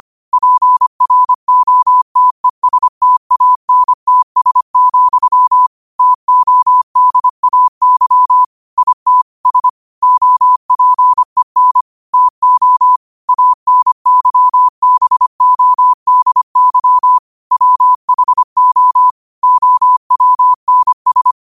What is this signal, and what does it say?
Morse code, heard as a keyed tone, about 25 wpm, 1000 Hz. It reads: PROTESTANTS, TODAY IT S OPEN TO ANYBODY WHO OWNS